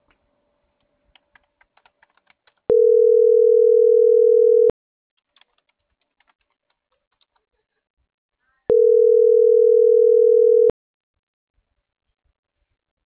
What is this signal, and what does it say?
Ringback tone (North America), 440 + 480 Hz.